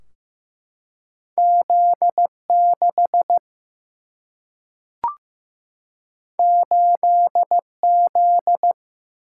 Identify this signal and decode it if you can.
Morse code, heard as a keyed tone, about 15 wpm, 700 Hz. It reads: Z6 8Z
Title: Single-Letter Single-Number 15wpm